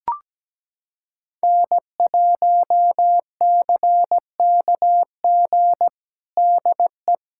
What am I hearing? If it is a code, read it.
Morse code, heard as a keyed tone, about 17 wpm, 700 Hz. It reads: N1CKG DE
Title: USA Callsign DE Callsign - 5 Characters - 17wpm